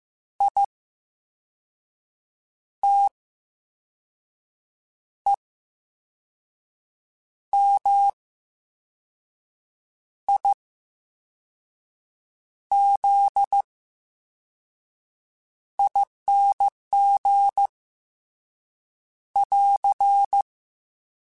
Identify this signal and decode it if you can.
Morse code, heard as a keyed tone, about 15 wpm, spaced as for 5 wpm, 800 Hz. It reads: ITEMIZING <AR>
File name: Special INGs / Special INGs-mp3s / Slow ing @13